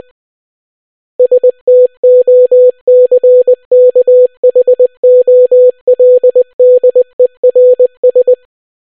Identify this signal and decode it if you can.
Morse code, heard as a keyed tone, about 20 wpm, 500 Hz. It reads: STOCKHOLDERS